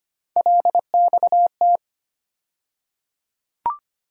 Morse code, heard as a keyed tone, about 25 wpm, 700 Hz. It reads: LXT
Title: USA Call Sign Postfixes - 3 Letters 25wpm